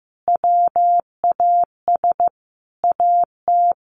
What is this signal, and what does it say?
Morse code, heard as a keyed tone, about 15 wpm, 700 Hz. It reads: WAS AT
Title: Sentences from Top 1300 Words 15wpm